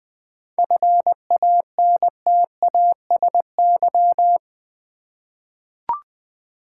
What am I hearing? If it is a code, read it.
Morse code, heard as a keyed tone, about 20 wpm, 700 Hz. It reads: FANTASY
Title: 7 Letter Words - First 3 Letters Given - 20wpm